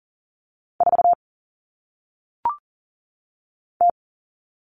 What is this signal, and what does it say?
Morse code, heard as a keyed tone, about 40 wpm, 700 Hz. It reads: 4 T
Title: All Cut Numbers - 1-Digit 40wpm